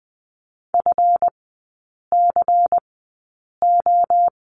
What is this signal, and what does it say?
Morse code, heard as a keyed tone, about 20 wpm, spaced as for 8 wpm, 700 Hz. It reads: FCO
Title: 138 Lesson 23e - Letter f - Three Characters 20wpm